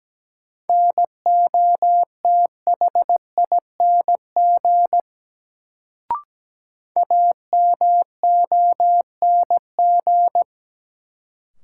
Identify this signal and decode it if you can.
Morse code, heard as a keyed tone, about 17 wpm, 700 Hz. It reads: NOTHING AMONG
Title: Sentences from Top 500 Words - New Word Drill - 17wpm